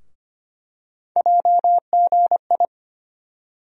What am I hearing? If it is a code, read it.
Morse code, heard as a keyed tone, about 25 wpm, 700 Hz. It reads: JGI
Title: USA Call Sign Postfixes - 3 Letters 25wpm